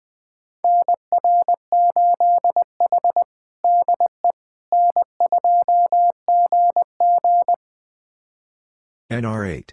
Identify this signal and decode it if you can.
Morse code, heard as a keyed tone, about 20 wpm, 700 Hz. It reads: NR8H DE N2GG
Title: USA Callsign DE Callsign - 4 Characters - 20wpm